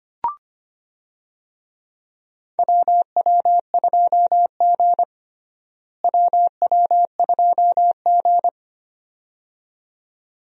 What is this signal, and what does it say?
Morse code, heard as a keyed tone, about 25 wpm, 700 Hz. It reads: WW2G WW2G
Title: USA Call Signs - 4 Characters - Repeated 2x - 25wpm